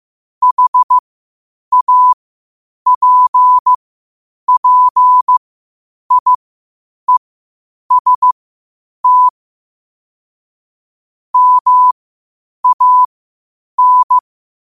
Morse code, heard as a keyed tone, about 15 wpm, spaced as for 8 wpm, 1000 Hz. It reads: HAPPIEST MAN
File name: qotdfiles / 8wpm